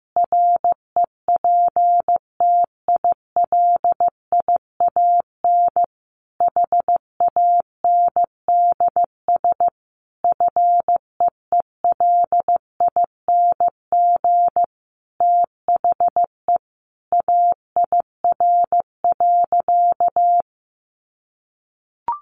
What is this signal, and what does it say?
Morse code, heard as a keyed tone, about 15 wpm, 700 Hz. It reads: REPTILIAN HANDS FEELING THE AIR.
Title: A Sound Of Thunder 15wpm